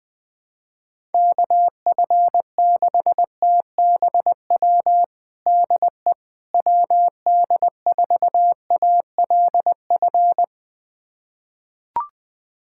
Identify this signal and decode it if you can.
Morse code, heard as a keyed tone, about 19 wpm, 700 Hz. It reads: KF6TBW DE WD4ALF
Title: USA Callsign DE Callsign - 6 Characters - 20wpm